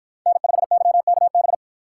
Morse code, heard as a keyed tone, about 53 wpm, 700 Hz. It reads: N5YKB